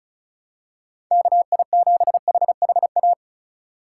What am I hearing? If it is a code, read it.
Morse code, heard as a keyed tone, about 36 wpm, 700 Hz. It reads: KI7HHA
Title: USA Call Signs - 6 Characters - 3x Character Spacing - 35wpm